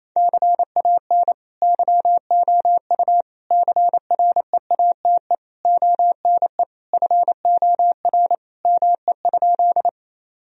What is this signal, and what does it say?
Morse code, heard as a keyed tone, about 27 wpm, 700 Hz. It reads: CAN YOU CREATE ONE FOR ME?
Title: Sentences from Top 1000 Words - Review Part 3 - 28wpm